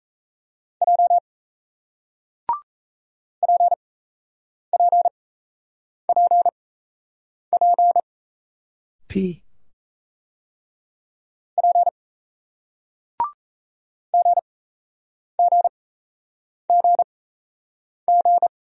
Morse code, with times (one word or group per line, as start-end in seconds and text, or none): 0.8-1.2 J
3.4-8.0 PPPP
11.6-11.9 P
14.1-18.5 GGGG